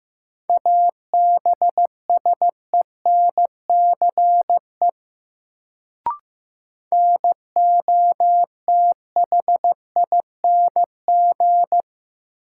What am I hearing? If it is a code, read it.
Morse code, heard as a keyed tone, about 15 wpm, 700 Hz. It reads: ABSENCE NOTHING
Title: Sets of 4 Words - New Word Drill - 15wpm